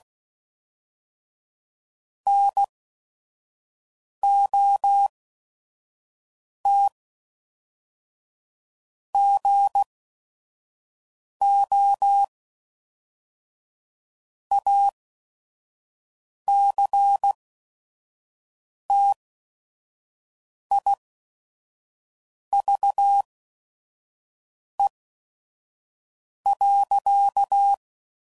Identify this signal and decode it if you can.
Morse code, heard as a keyed tone, about 16 wpm, spaced as for 5 wpm, 800 Hz. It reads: NOT GO ACTIVE.
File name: CW Code Groups @5WPM / Technical Subjects / OR Gate